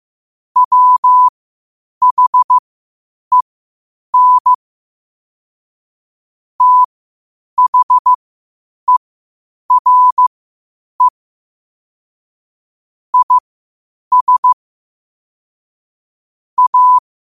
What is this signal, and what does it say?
Morse code, heard as a keyed tone, about 15 wpm, spaced as for 8 wpm, 1000 Hz. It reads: WHEN THERE IS A